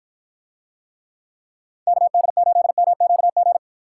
Morse code, heard as a keyed tone, about 53 wpm, 700 Hz. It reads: KD8MQG